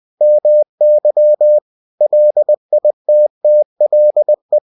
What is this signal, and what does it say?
Morse code, heard as a keyed tone, about 20 wpm, 600 Hz. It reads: MY LITTLE
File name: cw-books / winnie-the-pooh / winnie-20wpm-page-0106